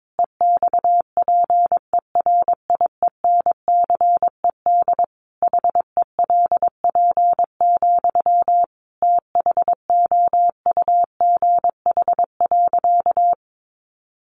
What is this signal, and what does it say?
Morse code, heard as a keyed tone, about 22 wpm, 700 Hz. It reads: EXPERIENCED HELP, THOUGH.
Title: Sentences from Top 1100 Words 22wpm